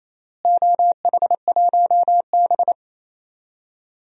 Morse code, heard as a keyed tone, about 28 wpm, 700 Hz. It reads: OH1B
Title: DXCC-100 Call Signs - 4 Characters - Repeated 3x - 28wpm